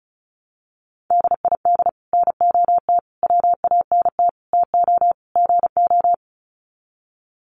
Morse code, heard as a keyed tone, about 34 wpm, 700 Hz. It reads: DID NOT WANT TO GO
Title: Sets of 5 Words 35wpm